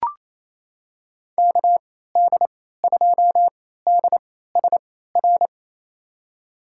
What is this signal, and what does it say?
Morse code, heard as a keyed tone, about 28 wpm, spaced as for 16 wpm, 700 Hz. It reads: KD2DSR
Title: USA Call Signs - 6 Characters - 3x Character Spacing - 28wpm